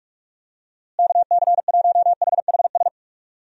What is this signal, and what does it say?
Morse code, heard as a keyed tone, about 43 wpm, 700 Hz. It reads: KC1HHS
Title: USA Call Signs - 6 Characters - Repeated 3x - 45wpm